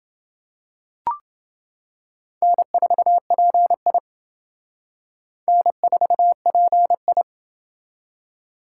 Morse code, heard as a keyed tone, about 28 wpm, 700 Hz. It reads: N4PI N4PI